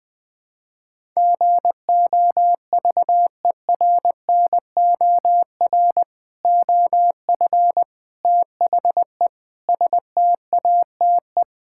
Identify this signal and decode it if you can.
Morse code, heard as a keyed tone, about 20 wpm, 700 Hz. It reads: GOVERNOR OF THE STATE
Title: Sentences with 3-Word Phrases - 20wpm